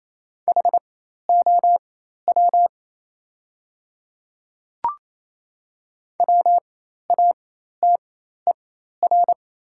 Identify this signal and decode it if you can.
Morse code, heard as a keyed tone, about 28 wpm, spaced as for 13 wpm, 700 Hz. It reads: HOW WATER
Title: 4x Character Spacing - Top 100 Words - 28wpm